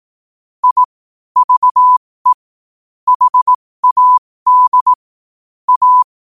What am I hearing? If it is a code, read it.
Morse code, heard as a keyed tone, about 18 wpm, spaced as for 15 wpm, 1000 Hz. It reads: I VE HAD A
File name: qotdfiles / 15wpm